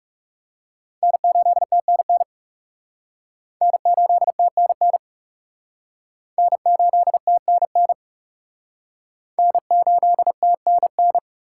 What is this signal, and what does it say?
Morse code, heard as a keyed tone, about 40 wpm, 700 Hz. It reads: N8TNN N8TNN N8TNN N8TNN